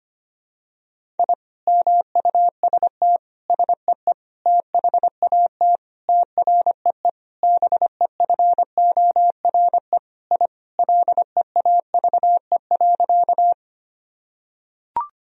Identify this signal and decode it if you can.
Morse code, heard as a keyed tone, about 25 wpm, 700 Hz. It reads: I MUST SEE THAT TREE BEFORE I LEAVE.
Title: Sentences from Top 1000 Words - Review Part 1 - 25wpm